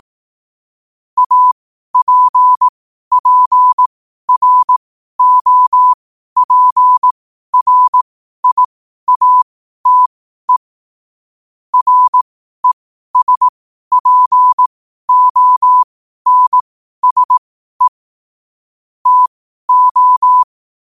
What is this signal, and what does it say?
Morse code, heard as a keyed tone, about 18 wpm, spaced as for 12 wpm, 1000 Hz. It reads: APPROPRIATE RESPONSE TO
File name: qotdfiles / 12wpm